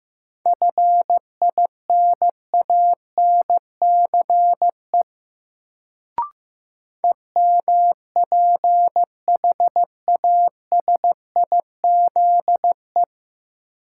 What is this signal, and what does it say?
Morse code, heard as a keyed tone, about 16 wpm, 700 Hz. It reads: FINANCE EMPHASIZE
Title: Sentences from Top 1500 Words - New Word Drill - 15wpm